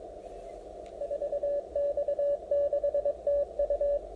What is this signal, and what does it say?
Morse code, heard as a keyed tone, about 22 wpm, 570 Hz, with radio noise. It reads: 4X6TU